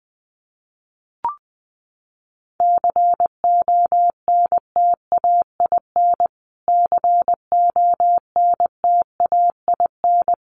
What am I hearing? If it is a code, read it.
Morse code, heard as a keyed tone, about 20 wpm, 700 Hz. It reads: CONTAIN CONTAIN
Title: Top 500 Words Repeated 20wpm